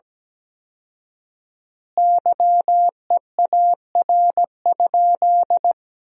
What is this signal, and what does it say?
Morse code, heard as a keyed tone, about 17 wpm, 700 Hz. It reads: YEAR?
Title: 4x Word-Spacing - Sentences from Top 200 Words - 17wpm